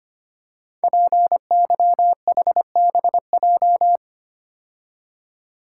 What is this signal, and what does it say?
Morse code, heard as a keyed tone, about 25 wpm, 700 Hz. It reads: PYHBJ
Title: Code Groups - 5 Letters - 25wpm